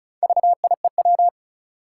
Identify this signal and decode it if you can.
Morse code, heard as a keyed tone, about 36 wpm, 700 Hz. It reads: VIEW
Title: Four Letter Words - 35wpm